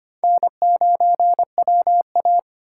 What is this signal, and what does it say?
Morse code, heard as a keyed tone, about 25 wpm, 700 Hz. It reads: N9WA
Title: USA Call Signs - 4 Characters - 3x Character Spacing - 25wpm